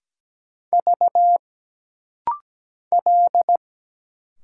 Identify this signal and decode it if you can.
Morse code, heard as a keyed tone, about 17 wpm, 700 Hz. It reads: V L
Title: Single Letter-Number - Rapid-Fire 17wpm